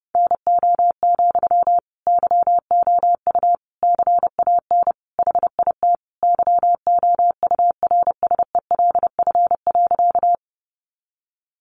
Morse code, heard as a keyed tone, about 30 wpm, 700 Hz. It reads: NO, YOU CAN HIT YOURSELF.
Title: Sentences from Top 1200 Words 30wpm